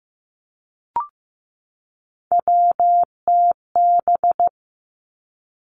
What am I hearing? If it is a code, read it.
Morse code, heard as a keyed tone, about 15 wpm, 700 Hz. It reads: WTB
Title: Three Challenging Letter-Number Combinations 15wpm